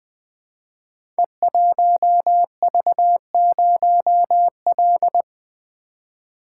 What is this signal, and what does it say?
Morse code, heard as a keyed tone, about 20 wpm, 700 Hz. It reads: E1V0L
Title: Code Groups - 5 Letter:Numbers - 20wpm